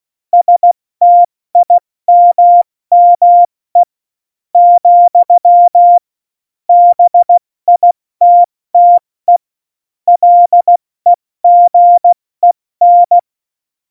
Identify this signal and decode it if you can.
Morse code, heard as a keyed tone, about 16 wpm, spaced as for 13 wpm, 700 Hz. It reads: STIMME, BITTE LEGEN
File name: Morsekurs / Witze / Witz 02. Oktober 2018, 70 F80